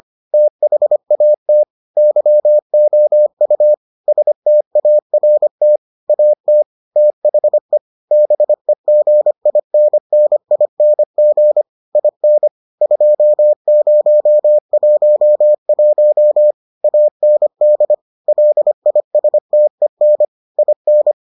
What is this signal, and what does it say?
Morse code, heard as a keyed tone, about 25 wpm, 600 Hz. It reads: THAT YOU START AT THE BEGINNING IN 2011 AND LISTEN IN